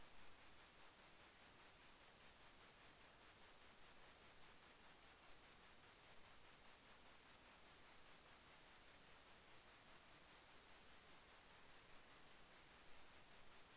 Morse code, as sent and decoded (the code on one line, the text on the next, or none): none